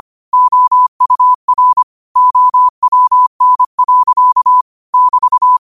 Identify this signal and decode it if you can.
Morse code, heard as a keyed tone, about 25 wpm, 1000 Hz. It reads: OUR OWN. <BT>
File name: qotdfiles / 25wpm